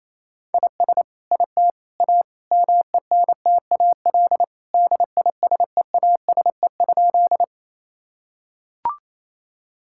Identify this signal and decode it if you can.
Morse code, heard as a keyed tone, about 28 wpm, 700 Hz. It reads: IS IT A MENTAL DISEASE?
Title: Sentences from Top 1000 Words - Review Part 6 - 28wpm